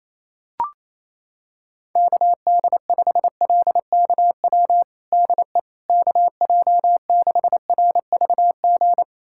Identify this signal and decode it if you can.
Morse code, heard as a keyed tone, about 28 wpm, 700 Hz. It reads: KD5LKW DE KJ6RVG
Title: USA Callsign DE Callsign - 6 Characters - 28wpm